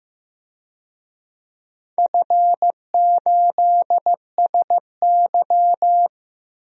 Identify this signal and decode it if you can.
Morse code, heard as a keyed tone, about 16 wpm, 700 Hz. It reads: F8SY
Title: DXCC-100 Call Signs - 4 Characters - 15wpm